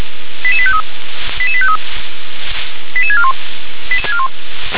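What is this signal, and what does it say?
Five-tone selective call (ZVEI), heard as quick runs of five tones, tones of about 70 ms. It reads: 88853 88853 88641
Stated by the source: ZVEI five-tone call 88985 received on an SDR.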